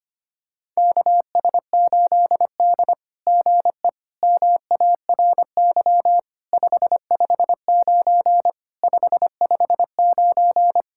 Morse code, heard as a keyed tone, about 25 wpm, 700 Hz. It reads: KS8D GE MARY 559 559
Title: SOTA QSO Practice 25wpm